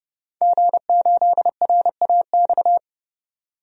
Morse code, heard as a keyed tone, about 32 wpm, 700 Hz. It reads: G8RAX
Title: DXCC-100 Call Signs - 5 Characters - 30wpm